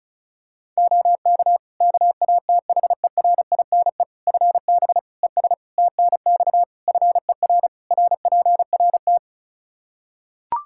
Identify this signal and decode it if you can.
Morse code, heard as a keyed tone, about 36 wpm, 700 Hz. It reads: OK KATHERINE FB ES TNX FER RPRT
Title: K1USN SST QSO Practice 35wpm